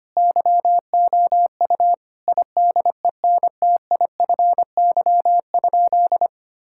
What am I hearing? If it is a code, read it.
Morse code, heard as a keyed tone, about 25 wpm, 700 Hz. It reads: YOU IDENTIFY?
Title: Sentences from Top 1000 Words - Review Part 6 - 25wpm